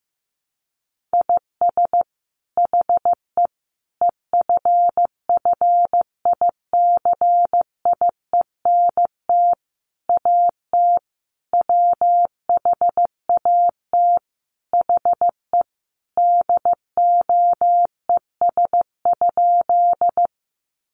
Morse code, heard as a keyed tone, about 15 wpm, 700 Hz. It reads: IS HE EFFICIENT AT WHAT HE DOES?
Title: Sentences from Top 1300 Words 15wpm